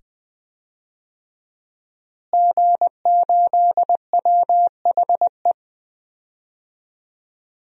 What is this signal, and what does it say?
Morse code, heard as a keyed tone, about 21 wpm, 700 Hz. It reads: G8WHE
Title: DXCC-100 Call Signs - 5 Characters - Repeated 2x - 20wpm